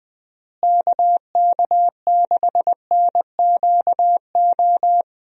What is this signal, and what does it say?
Morse code, heard as a keyed tone, about 20 wpm, 700 Hz. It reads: KK6NQO
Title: USA Call Signs - 6 Characters - Repeated 2x - 20wpm